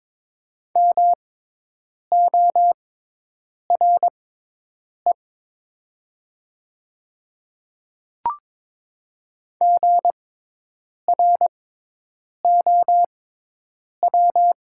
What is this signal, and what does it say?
Morse code, heard as a keyed tone, about 22 wpm, spaced as for 8 wpm, 700 Hz. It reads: MORE GROW
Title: 6x Character Spacing - Top 300 Words - 22wpm